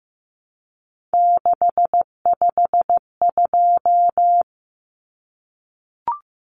Morse code, heard as a keyed tone, about 15 wpm, 700 Hz. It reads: 652
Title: Common Cut Numbers - 3-Digit 15wpm